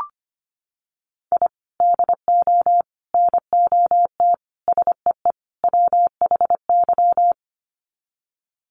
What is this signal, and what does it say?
Morse code, heard as a keyed tone, about 25 wpm, 700 Hz. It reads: I DO NOT SEE WHY
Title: Sets of 5 Words Encore 25wpm